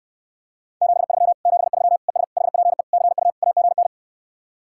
Morse code, heard as a keyed tone, about 68 wpm, 700 Hz. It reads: 73 73 ES HPE CU AGN